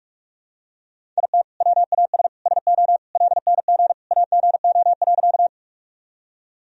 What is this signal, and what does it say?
Morse code, heard as a keyed tone, about 46 wpm, 700 Hz. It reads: IT WAS SO LNG AGO.
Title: Sentences from Top 500 Words and Abbreviations 45wpm